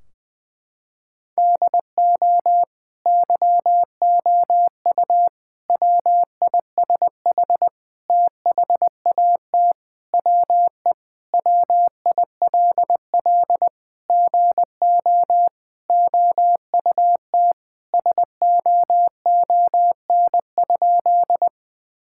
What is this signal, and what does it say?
Morse code, heard as a keyed tone, about 21 wpm, 700 Hz. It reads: DO YOU WISH THAT WE WILL GO OUT SOON?
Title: Sentences from Top 1600 Words 20wpm